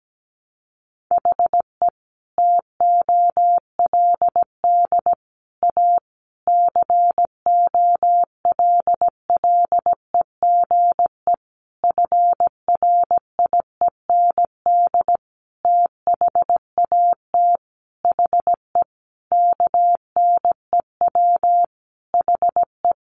Morse code, heard as a keyed tone, about 17 wpm, 700 Hz. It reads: HE TOLD A COLLEGE FRIEND THAT HE KNEW HE
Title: Sentences from Top 1300 Words 17wpm